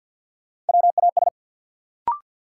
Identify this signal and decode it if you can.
Morse code, heard as a keyed tone, about 48 wpm, 700 Hz. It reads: WAS
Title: Sentences from Top 100 Words - New Word Drill - 50wpm